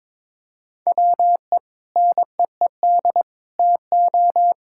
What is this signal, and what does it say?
Morse code, heard as a keyed tone, about 22 wpm, 700 Hz. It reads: WE NEED TO
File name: Sentences with 7-Word Phrases - 22wpm